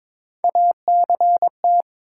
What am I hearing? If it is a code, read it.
Morse code, heard as a keyed tone, about 23 wpm, 700 Hz. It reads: ACT
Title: Three Letter Words - 22wpm